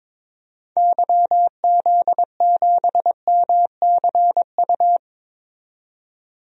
Morse code, heard as a keyed tone, about 22 wpm, 700 Hz. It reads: YZ7MCU
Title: DXCC-100 Call Signs - 6 Characters - 22wpm